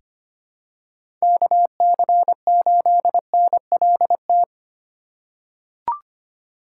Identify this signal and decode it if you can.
Morse code, heard as a keyed tone, about 25 wpm, 700 Hz. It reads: KC8NLT
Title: USA Call Signs - 8x Character Spacing - 25wpm